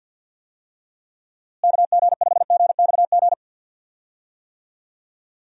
Morse code, heard as a keyed tone, about 49 wpm, 700 Hz. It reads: KG5GXG
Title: USA Call Signs - 6 Characters - Repeated 2x - 50wpm